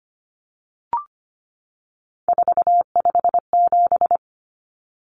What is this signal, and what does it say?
Morse code, heard as a keyed tone, about 25 wpm, 690 Hz. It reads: 457
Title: Numbers - 3-Digit 25wpm